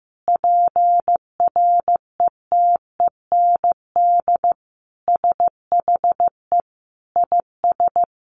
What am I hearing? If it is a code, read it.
Morse code, heard as a keyed tone, about 15 wpm, 700 Hz. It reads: PRETEND SHE IS
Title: Additional Easy Phrases 15wpm